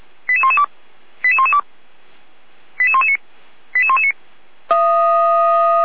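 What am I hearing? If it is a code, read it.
Five-tone selective call (ZVEI), heard as quick runs of five tones, tones of about 70 ms. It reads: 88112 88112 88119 88119
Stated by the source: ZVEI five-tone call received by SDR.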